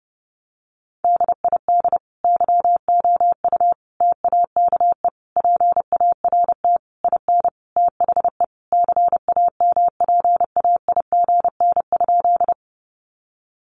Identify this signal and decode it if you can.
Morse code, heard as a keyed tone, about 30 wpm, 700 Hz. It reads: DID YOU TAKE PART IN THE CAMPAIGN?
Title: Sentences from Top 1200 Words 30wpm